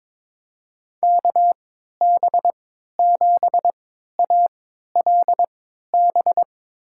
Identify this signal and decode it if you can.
Morse code, heard as a keyed tone, about 22 wpm, spaced as for 13 wpm, 700 Hz. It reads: KB7ALB
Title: USA Call Signs - 3x Character Spacing - 22wpm